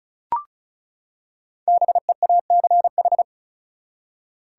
Morse code, heard as a keyed tone, about 36 wpm, 700 Hz. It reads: BEACH
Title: Five Letter Words - 35wpm